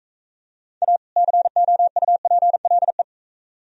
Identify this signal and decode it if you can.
Morse code, heard as a keyed tone, about 41 wpm, 700 Hz. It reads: A COUPLE